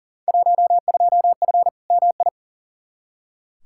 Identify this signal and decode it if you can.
Morse code, heard as a keyed tone, about 40 wpm, 700 Hz. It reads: 12F MI
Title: Field Day Exchanges Encore - 40wpm